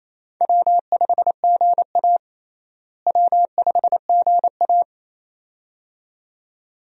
Morse code, heard as a keyed tone, about 28 wpm, 700 Hz. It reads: W5GA W5GA
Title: USA Call Signs - 4 Characters - Repeated 2x - 28wpm